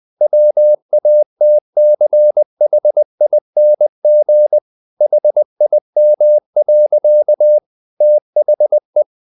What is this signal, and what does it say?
Morse code, heard as a keyed tone, about 20 wpm, 600 Hz. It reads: WATCHING HIM. THE
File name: the old man and the sea part 2 20wpm